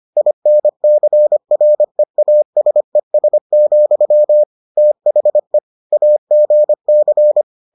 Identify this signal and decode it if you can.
Morse code, heard as a keyed tone, about 25 wpm, 600 Hz. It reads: INCREASES, THE AGC